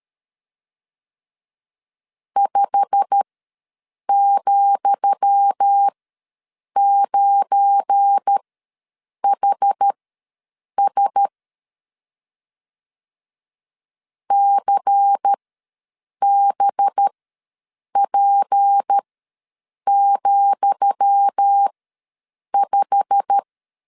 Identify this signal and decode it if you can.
Morse code, heard as a keyed tone, about 13 wpm, spaced as for 7 wpm, 800 Hz. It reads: 5,9HS CBP,5
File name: LEZIONI / 10 SIGNS / 10D COMA ws 12wpm 3618